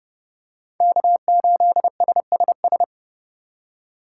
Morse code, heard as a keyed tone, about 28 wpm, 700 Hz. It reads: K8SSS